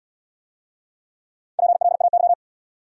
Morse code, heard as a keyed tone, about 78 wpm, 700 Hz. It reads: PLAY